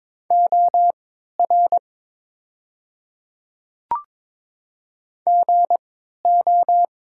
Morse code, heard as a keyed tone, about 21 wpm, spaced as for 12 wpm, 700 Hz. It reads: OR GO
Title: 3x Character Spacing - Top 100 Words - 22wpm